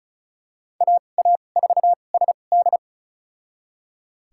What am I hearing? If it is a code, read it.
Morse code, heard as a keyed tone, about 36 wpm, spaced as for 25 wpm, 700 Hz. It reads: AA4SD
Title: USA Call Signs - 5 Characters - 2x Character Spacing - 35wpm